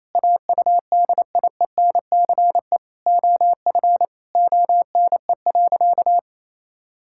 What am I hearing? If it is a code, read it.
Morse code, heard as a keyed tone, about 28 wpm, 700 Hz. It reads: AUDIENCE OF ONE.
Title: Sentences from Top 1000 Words - Review Part 5 - 28wpm